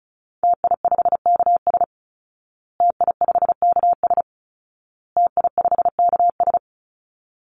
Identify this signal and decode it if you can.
Morse code, heard as a keyed tone, about 34 wpm, 700 Hz. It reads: TI5KS TI5KS TI5KS
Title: Call Signs Repeated 2x - 35wpm